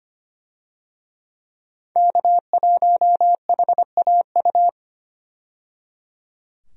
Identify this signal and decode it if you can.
Morse code, heard as a keyed tone, about 25 wpm, 700 Hz. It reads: K1HAU
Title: USA Call Signs - 5 Characters - 25wpm